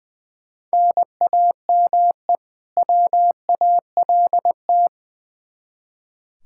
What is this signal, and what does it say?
Morse code, heard as a keyed tone, about 20 wpm, 700 Hz. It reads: NAME WALT
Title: Practice QSO - Exchange 1 - Beginning - 20wpm